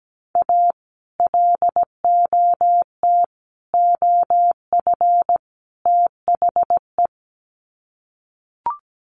Morse code, heard as a keyed tone, about 17 wpm, 700 Hz. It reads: A LOT OF THE
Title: Sets of 4 Words 17wpm